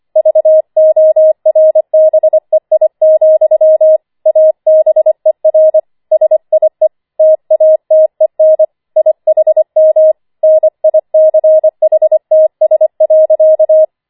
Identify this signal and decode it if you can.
Morse code, heard as a keyed tone, about 25 wpm, 630 Hz. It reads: VORBEI, ABER SIE TATEN IHM NICHTS.
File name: 25wpm